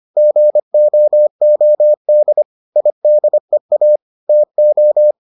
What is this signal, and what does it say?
Morse code, heard as a keyed tone, about 25 wpm, 600 Hz. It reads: GOOD IDEA TO